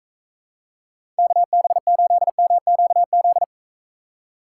Morse code, heard as a keyed tone, about 43 wpm, 700 Hz. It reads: KB8MQZ